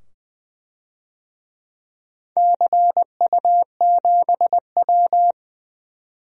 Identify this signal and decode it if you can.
Morse code, heard as a keyed tone, about 20 wpm, 700 Hz. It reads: CU7W
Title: DXCC-100 Call Signs - 4 Characters - 20wpm